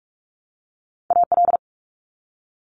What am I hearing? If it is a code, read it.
Morse code, heard as a keyed tone, about 46 wpm, 700 Hz. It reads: AL
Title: ARRL Sections 45wpm